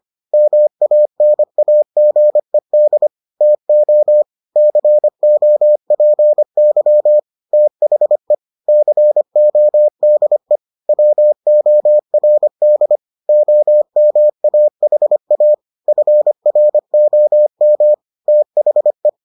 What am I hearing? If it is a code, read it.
Morse code, heard as a keyed tone, about 25 wpm, 600 Hz. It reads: MANAGED TO COPY THE CODE WORD OMAHA FROM THE